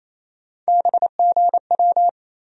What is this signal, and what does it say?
Morse code, heard as a keyed tone, about 28 wpm, 700 Hz. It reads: BGW
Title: USA Call Sign Postfixes - 3 Letters 28wpm